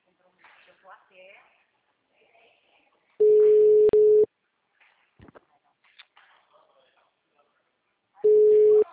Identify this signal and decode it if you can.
Ringback tone (Europe, 425 Hz), with speech.